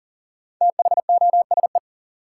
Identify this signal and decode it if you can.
Morse code, heard as a keyed tone, about 40 wpm, 700 Hz. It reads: THOSE
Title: Sentences from Top 300 Words - New Word Drill - 40wpm